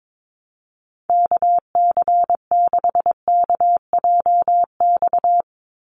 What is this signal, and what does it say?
Morse code, heard as a keyed tone, about 22 wpm, 700 Hz. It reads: KC6KJX
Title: USA Call Signs 22wpm